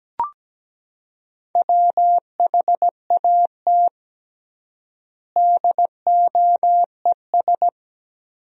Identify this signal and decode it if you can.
Morse code, heard as a keyed tone, about 17 wpm, 700 Hz. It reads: WHAT DOES
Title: 3x Word-Spacing - Sentences from Top 200 Words - 17wpm